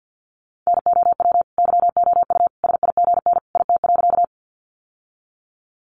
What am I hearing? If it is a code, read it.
Morse code, heard as a keyed tone, about 52 wpm, 700 Hz. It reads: NOW YOU SIGN IT.
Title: Sentences from Top 1000 Words 50wpm